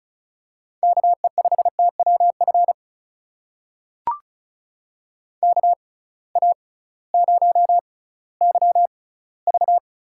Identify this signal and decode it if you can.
Morse code, heard as a keyed tone, about 36 wpm, spaced as for 19 wpm, 700 Hz. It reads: KE5TWF KA0YV
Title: USA Call Signs - 6 Characters - 6x Character Spacing - 35wpm